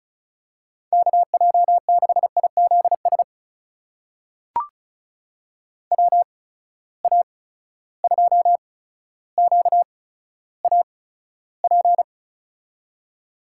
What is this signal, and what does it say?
Morse code, heard as a keyed tone, about 36 wpm, spaced as for 15 wpm, 700 Hz. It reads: KJ6IZS WA2QAP
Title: USA Call Signs - 6 Characters - 8x Character Spacing - 35wpm